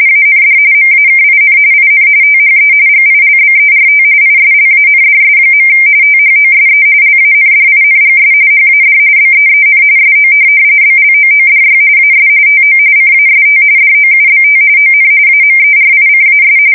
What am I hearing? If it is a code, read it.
RTTY, about 45.45 baud. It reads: AND MAKIN IN THE GILBERT ISLANDS. THOSE LANDINGS HAD BEEN SO BLOODY AS TO CAST DOUBT ON AMERICAS
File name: fsk170 / USS-New-Jersey